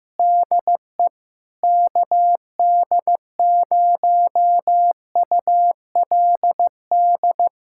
Morse code, heard as a keyed tone, about 15 wpm, 700 Hz. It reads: DE KD0ULD
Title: USA Callsign DE Callsign - 6 Characters - 15wpm